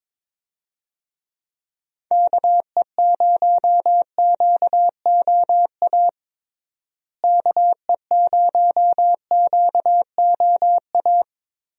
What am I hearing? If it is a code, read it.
Morse code, heard as a keyed tone, about 22 wpm, 700 Hz. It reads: KE0QOA KE0QOA
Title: USA Call Signs - 6 Characters - Repeated 3x - 22wpm